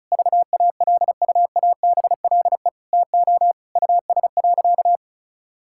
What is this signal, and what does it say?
Morse code, heard as a keyed tone, about 36 wpm, 700 Hz. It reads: VALUABLE TO US.
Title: Sentences from Top 1600 Words 35wpm